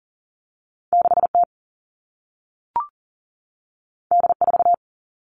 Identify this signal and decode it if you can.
Morse code, heard as a keyed tone, about 41 wpm, 700 Hz. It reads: 6T D4
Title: All Cut Numbers - 2-Digit 40wpm